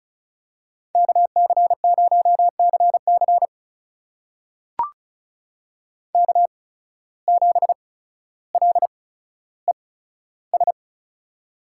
Morse code, heard as a keyed tone, about 36 wpm, spaced as for 15 wpm, 700 Hz. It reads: KC0CC K7LES
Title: USA Call Signs - 5 Characters - 8x Character Spacing - 35wpm